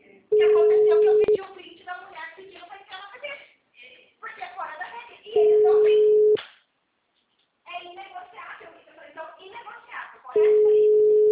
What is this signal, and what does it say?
Ringback tone (Europe, 425 Hz), with speech.